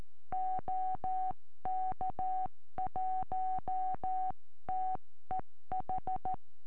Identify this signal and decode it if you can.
Morse code, heard as a keyed tone, about 13 wpm, 750 Hz. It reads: OK1TEH
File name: eme / sm4ive 70cm